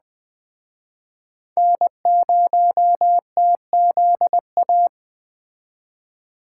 Morse code, heard as a keyed tone, about 21 wpm, 700 Hz. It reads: N0TZA